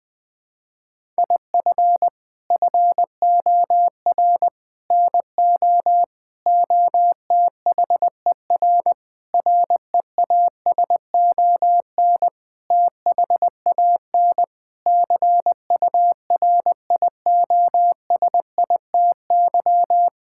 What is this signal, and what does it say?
Morse code, heard as a keyed tone, about 20 wpm, 700 Hz. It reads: IF FOR NO OTHER REASON THAN CURIOSITY
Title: Sentences with 6-Word Phrases - 20wpm